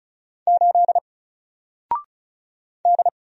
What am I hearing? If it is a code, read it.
Morse code, heard as a keyed tone, about 36 wpm, 700 Hz. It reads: 8 D
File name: Single-Letter-Number-35wpm